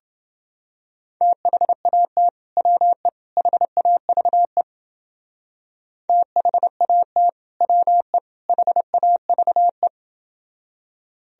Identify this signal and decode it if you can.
Morse code, heard as a keyed tone, about 29 wpm, 700 Hz. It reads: THAT WE HAVE THAT WE HAVE